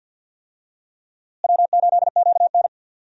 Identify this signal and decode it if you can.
Morse code, heard as a keyed tone, about 52 wpm, 700 Hz. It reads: W8QN
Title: USA Call Signs - 4 Characters - Repeated 3x - 50wpm